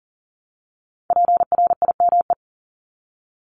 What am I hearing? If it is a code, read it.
Morse code, heard as a keyed tone, about 42 wpm, 700 Hz. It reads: PRIME
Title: Multi-letter Phonemes 40wpm